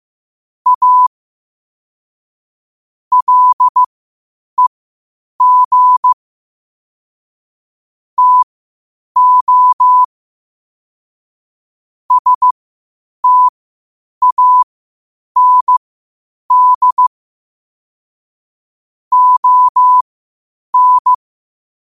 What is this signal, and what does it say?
Morse code, heard as a keyed tone, about 15 wpm, spaced as for 8 wpm, 1000 Hz. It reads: A LEG TO STAND ON
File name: qotdfiles / 8wpm